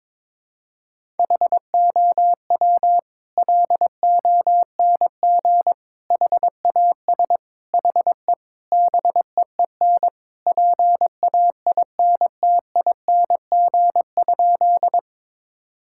Morse code, heard as a keyed tone, about 22 wpm, 700 Hz. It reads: HOW LONG HAS HE BEEN PAINTING?
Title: Sentences from Top 1000 Words - Review Part 6 - 22wpm